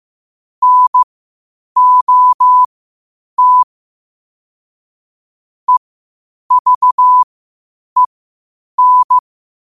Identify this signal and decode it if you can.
Morse code, heard as a keyed tone, about 15 wpm, spaced as for 8 wpm, 1000 Hz. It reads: NOT EVEN
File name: qotdfiles / 8wpm